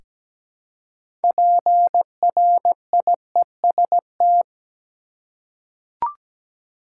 Morse code, heard as a keyed tone, about 17 wpm, 700 Hz. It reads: PRIEST
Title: Six Letter Words - 17wpm